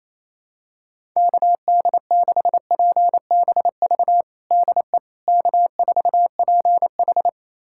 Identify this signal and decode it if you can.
Morse code, heard as a keyed tone, about 28 wpm, 700 Hz. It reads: KD6PBV DE K4PH
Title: USA Callsign DE Callsign 28wpm